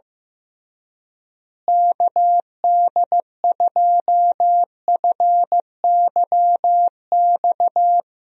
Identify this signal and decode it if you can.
Morse code, heard as a keyed tone, about 16 wpm, 700 Hz. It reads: KD2FYX
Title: USA Call Signs - 6 Characters - Repeated 2x - 15wpm